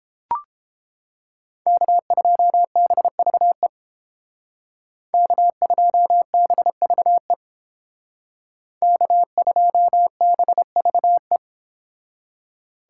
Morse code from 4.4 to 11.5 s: K2BVE K2BVE

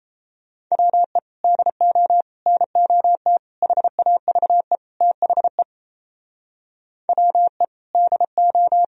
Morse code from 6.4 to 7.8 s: WE

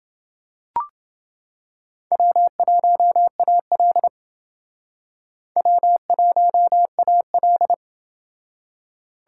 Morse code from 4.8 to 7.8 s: W1AL

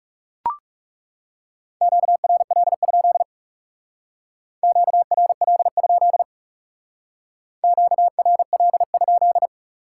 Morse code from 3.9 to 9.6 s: QRL? QRL?

